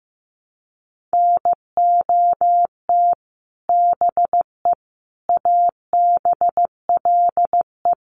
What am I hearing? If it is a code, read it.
Morse code, heard as a keyed tone, about 15 wpm, 700 Hz. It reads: NOT BE ABLE
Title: Sets of 4 Words 15wpm